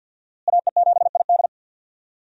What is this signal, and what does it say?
Morse code, heard as a keyed tone, about 50 wpm, 700 Hz. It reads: AE7ID
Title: USA Call Signs - 5 Characters - Repeated 3x - 50wpm